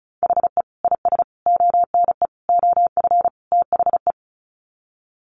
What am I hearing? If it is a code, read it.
Morse code, heard as a keyed tone, about 34 wpm, 700 Hz. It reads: HE IS ONE OF THE
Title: Sets of 5 Words Encore 35wpm